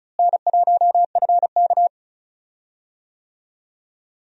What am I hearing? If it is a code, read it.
Morse code, heard as a keyed tone, about 36 wpm, 700 Hz. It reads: N1FK
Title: USA Call Signs - 4 Characters - Repeated 3x - 35wpm